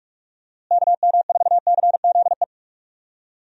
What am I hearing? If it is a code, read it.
Morse code, heard as a keyed tone, about 46 wpm, 700 Hz. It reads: KM4CZE
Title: USA Call Signs - 6 Characters - Repeated 3x - 45wpm